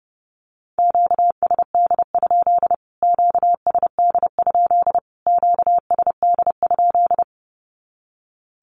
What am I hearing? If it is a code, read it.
Morse code, heard as a keyed tone, about 30 wpm, 700 Hz. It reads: QSD? QSD? QSD?
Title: Q-Codes Repeated 3x 30wpm